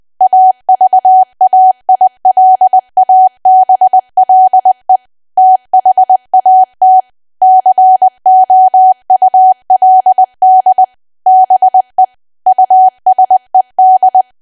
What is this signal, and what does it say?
Morse code, heard as a keyed tone, about 20 wpm, 750 Hz. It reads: AVAILABLE THAT COULD BE USED
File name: Archive / 20 WPM / 20WPM